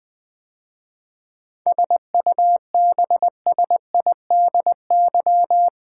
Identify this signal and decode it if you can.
Morse code, heard as a keyed tone, about 20 wpm, 700 Hz. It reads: SUBSIDY
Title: 7 Letter Words - First 3 Letters Given - 20wpm